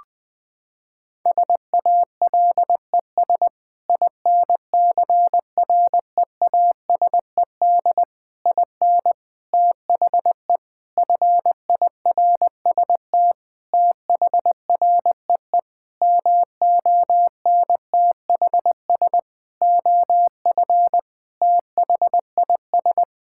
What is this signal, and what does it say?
Morse code, heard as a keyed tone, about 20 wpm, 700 Hz. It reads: SALES INCREASED IN THE FIRST THREE MONTHS OF THIS
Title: Sentences with 8-Word Phrases - 20wpm